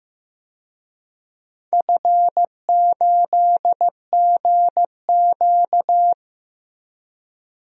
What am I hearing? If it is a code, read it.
Morse code, heard as a keyed tone, about 15 wpm, 700 Hz. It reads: F8GQ
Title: DXCC-100 Call Signs - 4 Characters - 15wpm